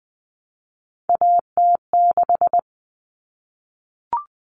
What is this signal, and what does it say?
Morse code, heard as a keyed tone, about 20 wpm, 700 Hz. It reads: AT6
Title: All Cut Numbers - 3-Digit 20wpm